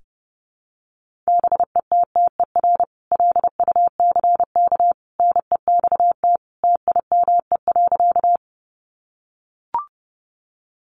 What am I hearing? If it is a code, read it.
Morse code, heard as a keyed tone, about 30 wpm, 700 Hz. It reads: BETTER LUCK NEXT TIME.
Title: Sentences from Top 1500 Words 30wpm